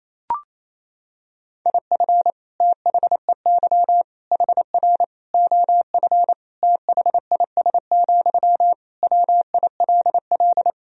Morse code, heard as a keyed tone, about 28 wpm, 700 Hz. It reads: IF THEY HR OF THIS, WILL
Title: Sentences from Top 400 Words and Abbreviations 28wpm